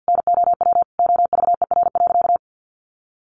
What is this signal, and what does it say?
Morse code, heard as a keyed tone, about 50 wpm, 700 Hz. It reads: NOW OVER.
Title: Sentences from Top 1100 Words 50wpm